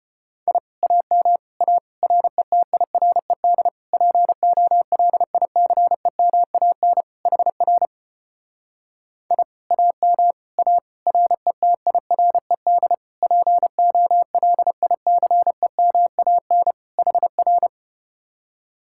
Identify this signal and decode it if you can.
Morse code, heard as a keyed tone, about 33 wpm, 700 Hz. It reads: I AM A RETIRED POLICEMAN HR I AM A RETIRED POLICEMAN HR